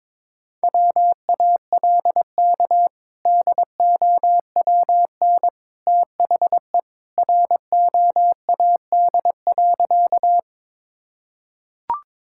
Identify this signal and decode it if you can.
Morse code, heard as a keyed tone, about 21 wpm, 700 Hz. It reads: WALK DOWN THE ROAD.
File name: practice / 22wpm / Sentences with 2-Word Phrases - 22wpm